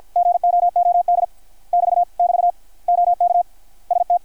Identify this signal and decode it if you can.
Morse code, heard as a keyed tone, about 53 wpm, 660 Hz. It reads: GOOD <BT> <BT> OK ST
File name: archived / CRSnet